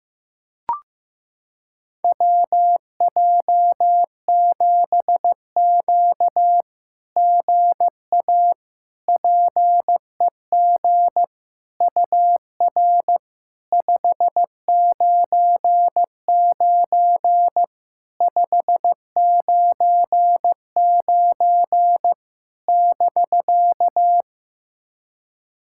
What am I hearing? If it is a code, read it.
Morse code, heard as a keyed tone, about 15 wpm, 700 Hz. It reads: WJ7Q GA PEG UR 599 599 <BK>
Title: SOTA QSO Practice 15wpm